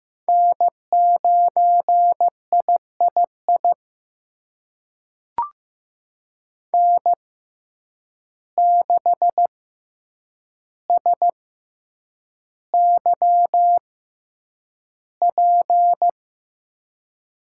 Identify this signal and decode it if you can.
Morse code, heard as a keyed tone, about 15 wpm, spaced as for 8 wpm, 700 Hz. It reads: N9III N6SYP
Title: USA Call Signs - 5 Characters - 6x Character Spacing - 15wpm